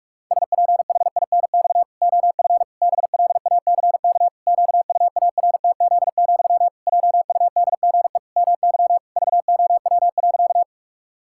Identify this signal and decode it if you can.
Morse code, heard as a keyed tone, about 41 wpm, 700 Hz. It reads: SPHINX OF BLACK QUARTZ, JUDGE MY VOW.
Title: Pangram Challenge v2 - 45wpm